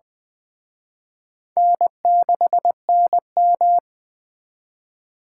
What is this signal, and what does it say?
Morse code, heard as a keyed tone, about 20 wpm, 700 Hz. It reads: N6NM
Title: Speed Racing - USA 1x2 Call Signs 17wpm